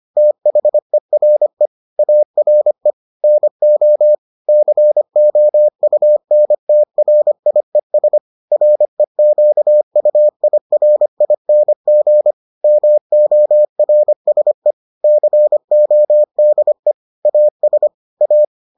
Morse code, heard as a keyed tone, about 25 wpm, 600 Hz. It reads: THERE ARE NO COUNTRIES REQUIRING MORSE CODE AS A